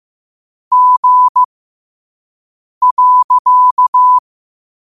Morse code, heard as a keyed tone, about 15 wpm, spaced as for 5 wpm, 1000 Hz. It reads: G.
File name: qotdfiles / 5wpm